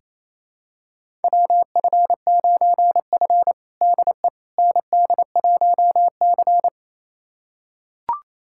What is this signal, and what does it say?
Morse code, heard as a keyed tone, about 27 wpm, 690 Hz. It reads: WF9F DE ND1C
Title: USA Callsign DE Callsign - 4 Characters - 28wpm